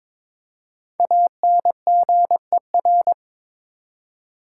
Morse code, heard as a keyed tone, about 22 wpm, 700 Hz. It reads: ANGER
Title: Five Letter Words - 22wpm